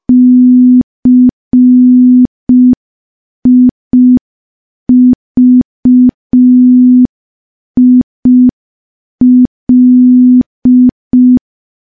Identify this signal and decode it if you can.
Morse code, heard as a keyed tone, about 5 wpm, 250 Hz. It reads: CIVIL
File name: morse / morse-training-05WPM